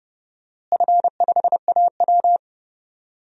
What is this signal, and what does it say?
Morse code, heard as a keyed tone, about 30 wpm, 700 Hz. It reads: F5AW
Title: DXCC-100 Call Signs - 4 Characters - Repeated 3x - 30wpm